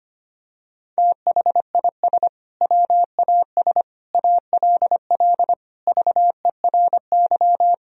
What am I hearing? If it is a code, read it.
Morse code, heard as a keyed tone, about 25 wpm, 700 Hz. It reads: THIS WAS ALL VERY
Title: Sentences from Top 1000 Words - Review Part 3 - 25wpm